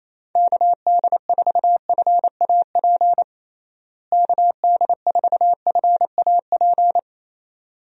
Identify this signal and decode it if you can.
Morse code, heard as a keyed tone, about 28 wpm, 700 Hz. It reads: KD4FAP KD4FAP
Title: USA Call Signs - 6 Characters - Repeated 2x - 28wpm